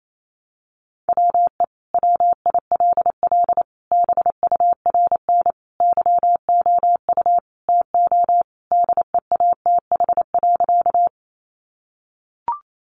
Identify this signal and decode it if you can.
Morse code, heard as a keyed tone, about 28 wpm, 700 Hz. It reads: WE WILL BURN YOU TO DEATH.
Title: Sentences from Top 1300 Words 28wpm